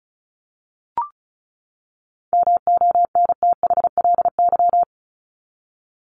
Morse code, heard as a keyed tone, about 34 wpm, 700 Hz. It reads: MONTHLY
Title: LY Suffix - Top 100 Words with Base Words 35wpm